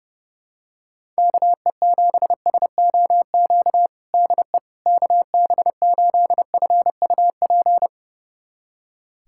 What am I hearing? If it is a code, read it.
Morse code, heard as a keyed tone, about 31 wpm, 700 Hz. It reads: KE7SOQ DE KB8FUP
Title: USA Callsign DE Callsign 30wpm